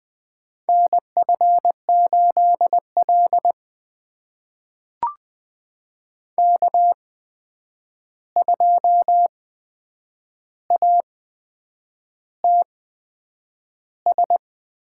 Morse code, heard as a keyed tone, about 20 wpm, spaced as for 8 wpm, 700 Hz. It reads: NF8L K2ATS
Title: USA Call Signs - 8x Character Spacing - 20wpm